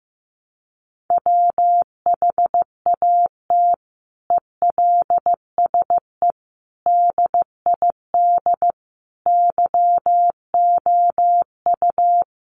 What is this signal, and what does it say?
Morse code, heard as a keyed tone, about 15 wpm, 700 Hz. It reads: WHAT ELSE DID YOU
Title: Sentences from Top 800 Words 15wpm